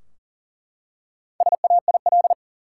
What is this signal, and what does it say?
Morse code, heard as a keyed tone, about 40 wpm, 700 Hz. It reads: SAIL
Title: Four Letter Words - 40wpm